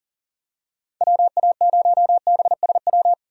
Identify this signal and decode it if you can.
Morse code, heard as a keyed tone, about 40 wpm, 700 Hz. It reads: WA0BSW
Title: USA Call Signs - 6 Characters - 40wpm